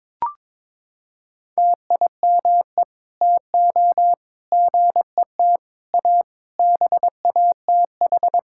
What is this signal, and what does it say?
Morse code, heard as a keyed tone, about 22 wpm, 700 Hz. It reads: TIME TO GET A BATH
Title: Sentences from Top 1600 Words 22wpm